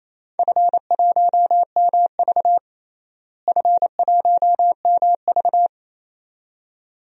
Morse code, heard as a keyed tone, about 28 wpm, 700 Hz. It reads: F1MV F1MV
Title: DXCC-100 Call Signs - 4 Characters - Repeated 2x - 28wpm